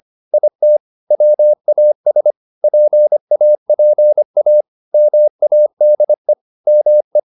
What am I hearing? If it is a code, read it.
Morse code, heard as a keyed tone, about 25 wpm, 600 Hz. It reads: IT WAS PAPA MADE ME
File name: the old man and the sea part 1 25wpm